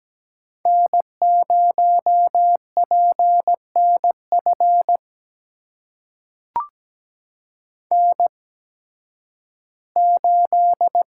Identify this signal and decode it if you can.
Morse code, heard as a keyed tone, about 17 wpm, spaced as for 11 wpm, 700 Hz. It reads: N0PNF N8
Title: USA Call Signs - 5 Characters - 8x Character Spacing - 17wpm